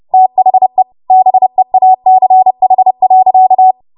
Morse code, heard as a keyed tone, about 30 wpm, 750 Hz. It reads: THE BEACH.
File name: Morse / 30WPM